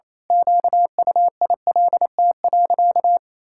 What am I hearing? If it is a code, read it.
Morse code, heard as a keyed tone, about 28 wpm, 700 Hz. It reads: QUILT.
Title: Pangram Challenge v2 - 28wpm